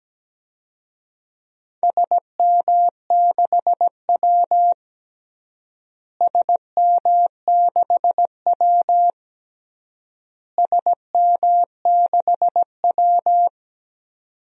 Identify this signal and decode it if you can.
Morse code, heard as a keyed tone, about 17 wpm, 700 Hz. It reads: SM6W SM6W SM6W
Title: DXCC-100 Call Signs - 4 Characters - Repeated 3x - 17wpm